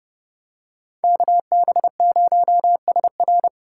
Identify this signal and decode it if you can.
Morse code, heard as a keyed tone, about 29 wpm, 700 Hz. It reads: KB0SR